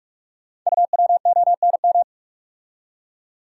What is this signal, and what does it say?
Morse code, heard as a keyed tone, about 44 wpm, 700 Hz. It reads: UWONM
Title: Code Groups - 5 Letters - 45wpm